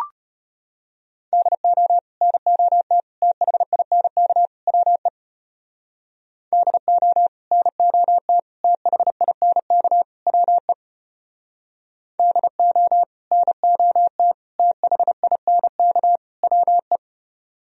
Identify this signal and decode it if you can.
Morse code, heard as a keyed tone, about 32 wpm, 700 Hz. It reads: DO NOT THINK WE DO NOT THINK WE DO NOT THINK WE